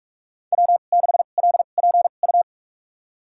Morse code, heard as a keyed tone, about 46 wpm, spaced as for 33 wpm, 700 Hz. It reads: W6LPU
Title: USA Call Signs - 2x Character Spacing - 45wpm